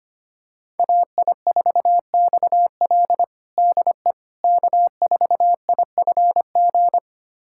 Morse code, heard as a keyed tone, about 25 wpm, 700 Hz. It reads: AI4XL DE K4IFG
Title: USA Callsign DE Callsign - 5 Characters - 25wpm